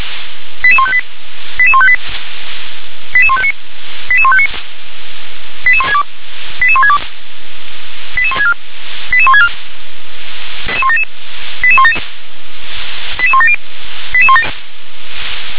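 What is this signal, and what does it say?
Five-tone selective call (ZVEI), heard as quick runs of five tones, tones of about 80 ms. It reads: 88158 88158 88159 88159 88162 88162 88164 88164 88177 88170
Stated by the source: Received on SDR as a ZVEI five-tone selective call.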